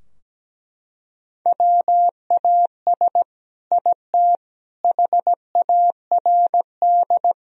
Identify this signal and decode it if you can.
Morse code, heard as a keyed tone, about 17 wpm, 700 Hz. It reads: WAS IT HARD
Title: Sentences from Top 1000 Words - Review Part 4 - 17wpm